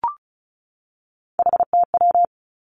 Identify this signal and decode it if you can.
Morse code, heard as a keyed tone, about 36 wpm, 700 Hz. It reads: HTW
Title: Three Challenging Letter-Number Combinations 35wpm